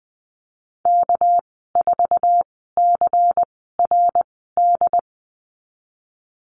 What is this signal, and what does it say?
Morse code, heard as a keyed tone, about 20 wpm, spaced as for 15 wpm, 700 Hz. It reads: K4CRD
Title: 231 Lesson 37g - Letter x - Call Signs 20wpm